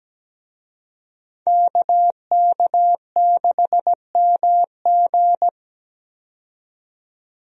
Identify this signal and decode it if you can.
Morse code, heard as a keyed tone, about 17 wpm, 700 Hz. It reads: KK6MG
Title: USA Call Signs - 5 Characters - 17wpm